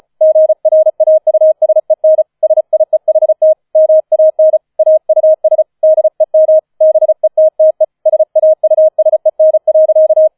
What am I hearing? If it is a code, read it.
Morse code, heard as a keyed tone, about 34 wpm, 630 Hz. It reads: GRAUSEN SIEHT MAN AUS DEM BETTE SAUSEN.
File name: 005 Fünfter Streich 35wpm